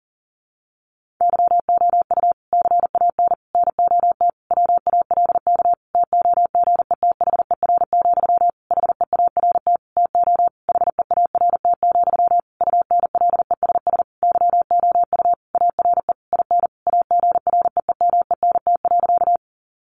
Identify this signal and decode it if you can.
Morse code, heard as a keyed tone, about 40 wpm, 700 Hz. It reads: YOU CAN NOT WALK TOGETHER, HEART TO HEART, UNLESS YOU ARE IN AGREEMENT.
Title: Sentences from Top 1500 Words 40wpm